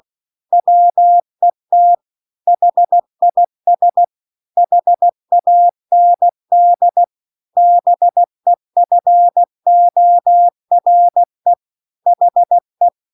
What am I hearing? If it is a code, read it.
Morse code, heard as a keyed tone, about 16 wpm, 700 Hz. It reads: WET HIS HAND BEFORE HE